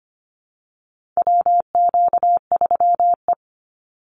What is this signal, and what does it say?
Morse code, heard as a keyed tone, about 25 wpm, 700 Hz. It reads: WQ3E
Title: Call Signs Repeated 2x - 25wpm